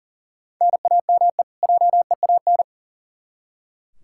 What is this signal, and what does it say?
Morse code, heard as a keyed tone, about 40 wpm, 700 Hz. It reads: NAME JEAN
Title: Practice QSO - Exchange 1 - Beginning - 40wpm